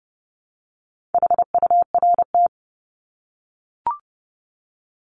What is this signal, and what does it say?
Morse code, heard as a keyed tone, about 30 wpm, 700 Hz. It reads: HURT
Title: Top 1751-2000 Words 30wpm